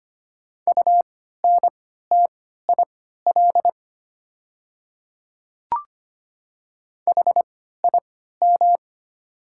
Morse code, heard as a keyed tone, about 25 wpm, spaced as for 14 wpm, 700 Hz. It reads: UNTIL HIM
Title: 3x Character Spacing - Top 300 Words - 25wpm